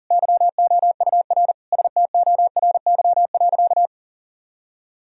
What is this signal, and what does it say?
Morse code, heard as a keyed tone, about 40 wpm, 700 Hz. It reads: YOUR STORY.
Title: Sentences from Top 1000 Words - Review Part 3 - 40wpm